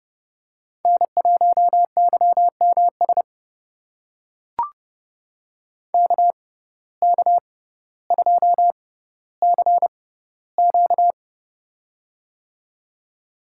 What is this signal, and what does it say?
Morse code, heard as a keyed tone, about 30 wpm, spaced as for 15 wpm, 700 Hz. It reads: N1YMS KK2CQ
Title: USA Call Signs - 5 Characters - 6x Character Spacing - 30wpm